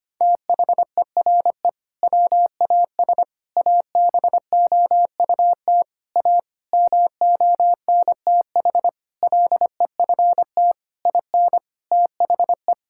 Morse code, heard as a keyed tone, about 25 wpm, 700 Hz. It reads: THERE WAS ABOUT A MONTH LEFT IN THE
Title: Sentences from Top 1000 Words - Review Part 5 - 25wpm